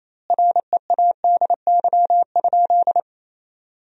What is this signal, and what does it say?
Morse code, heard as a keyed tone, about 28 wpm, 700 Hz. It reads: READY?
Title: 2x Word-Spacing - Sentences from Top 400 Words - 28wpm